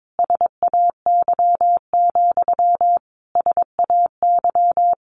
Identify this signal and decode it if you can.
Morse code, heard as a keyed tone, about 22 wpm, 700 Hz. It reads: SAY, SAY
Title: Sentences from Top 1000 Words 22wpm